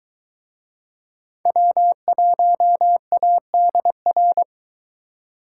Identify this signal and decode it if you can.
Morse code, heard as a keyed tone, about 22 wpm, 700 Hz. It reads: W1ADR